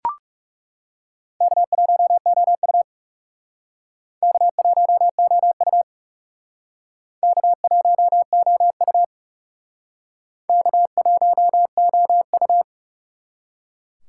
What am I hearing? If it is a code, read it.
Morse code, heard as a keyed tone, about 38 wpm, 700 Hz. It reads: K1OU K1OU K1OU K1OU
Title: Speed Racing - USA 1x2 Call Signs 30wpm